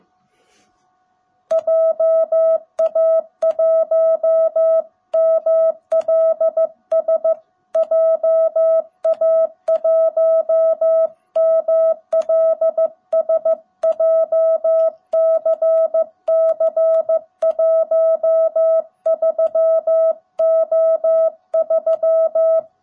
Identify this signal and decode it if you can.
Morse code, heard as a keyed tone, about 15 wpm, 660 Hz. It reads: JA1MLS JA1MLS JCC13O3
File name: jh8cbh / 2021cw / cw18b